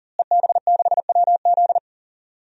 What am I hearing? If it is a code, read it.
Morse code, heard as a keyed tone, about 40 wpm, 700 Hz. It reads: EB6WZ
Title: DXCC-100 Call Signs - 5 Characters - Repeated 3x - 40wpm